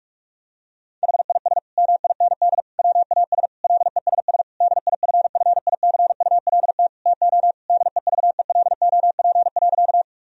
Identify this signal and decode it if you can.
Morse code, heard as a keyed tone, about 41 wpm, 700 Hz. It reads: HIS MIND WAS LESS DIFFICULT TO DEVELOP.
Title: Sentences from Top 1000 Words - Review Part 2 - 45wpm